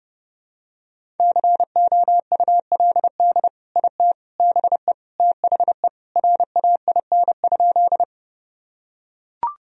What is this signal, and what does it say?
Morse code, heard as a keyed tone, about 32 wpm, 700 Hz. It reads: COULD IT BE THE RAIN?
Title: Sentences from Top 1000 Words - Review Part 1 - 30wpm